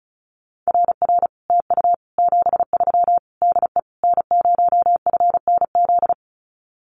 Morse code, heard as a keyed tone, about 34 wpm, 700 Hz. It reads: RR TU 73 DE N0FNZ
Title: Common QSO Phrases Encore 35wpm